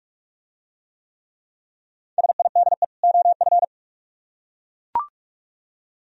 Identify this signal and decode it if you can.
Morse code, heard as a keyed tone, about 43 wpm, 700 Hz. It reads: SIDE OF